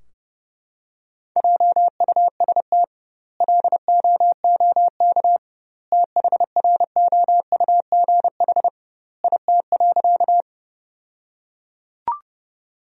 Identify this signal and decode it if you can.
Morse code, heard as a keyed tone, about 32 wpm, 700 Hz. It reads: JUST LOOK THROUGH IT.
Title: 2x Word-Spacing - Sentences from Top 200 Words - 30wpm